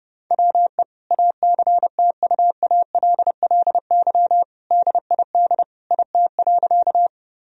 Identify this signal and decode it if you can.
Morse code, heard as a keyed tone, about 30 wpm, 700 Hz. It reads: WE ACTUALLY DID IT.
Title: Sentences from Top 1000 Words - Review Part 4 - 30wpm